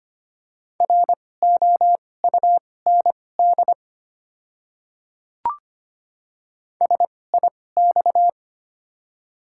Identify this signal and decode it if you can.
Morse code, heard as a keyed tone, about 25 wpm, spaced as for 18 wpm, 690 Hz. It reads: ROUND SIX
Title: 2x Character Spacing - Top 500 Words - 25wpm